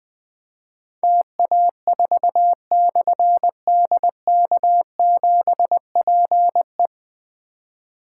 Morse code, heard as a keyed tone, about 20 wpm, 700 Hz. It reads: TA4/DK7PE
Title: Difficult Call Signs 20wpm